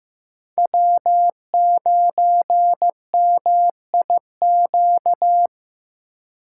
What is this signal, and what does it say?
Morse code, heard as a keyed tone, about 15 wpm, 700 Hz. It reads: W9MIQ
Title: USA Call Signs - 5 Characters - 4x Character Spacing - 15wpm